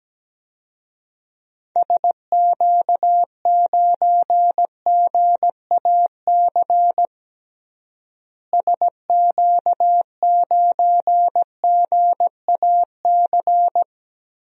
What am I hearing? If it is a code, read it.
Morse code, heard as a keyed tone, about 17 wpm, 700 Hz. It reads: SQ9GAC SQ9GAC
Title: DXCC-100 Call Signs - 6 Characters - Repeated 3x - 17wpm